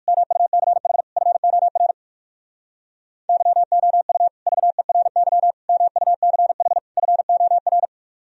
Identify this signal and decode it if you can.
Morse code, heard as a keyed tone, about 66 wpm, spaced as for 56 wpm, 700 Hz. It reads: MUCH FOR YOU VERY MUCH FOR
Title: Speed-Racing - Sets of 4 Words 45wpm